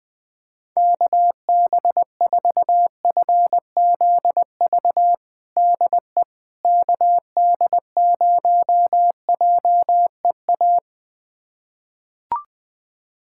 Morse code, heard as a keyed tone, about 21 wpm, 700 Hz. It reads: KB4FZV DE KD0JEA
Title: USA Callsign DE Callsign - 6 Characters - 20wpm